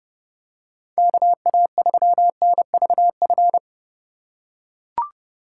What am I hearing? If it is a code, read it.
Morse code, heard as a keyed tone, about 30 wpm, 700 Hz. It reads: KA3NVF
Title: USA Call Signs - 6 Characters - 2x Character Spacing - 30wpm